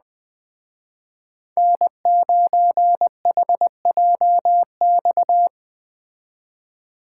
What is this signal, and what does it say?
Morse code, heard as a keyed tone, about 19 wpm, 700 Hz. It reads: N9HJX